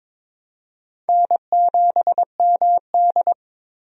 Morse code, heard as a keyed tone, about 22 wpm, 700 Hz. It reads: N7MD
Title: USA Call Signs - 4 Characters - 22wpm